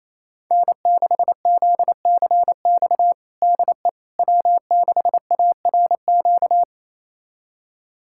Morse code, heard as a keyed tone, about 28 wpm, 700 Hz. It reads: N6ZCX DE W6ARQ
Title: USA Callsign DE Callsign - 5 Characters - 28wpm